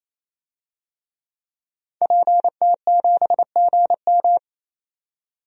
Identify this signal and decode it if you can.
Morse code, heard as a keyed tone, about 28 wpm, 700 Hz. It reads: PT7GM
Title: DXCC-100 Call Signs - 5 Characters - 28wpm